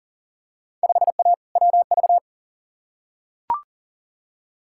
Morse code, heard as a keyed tone, about 40 wpm, 700 Hz. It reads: 5A WV
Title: Field Day Exchanges - 40wpm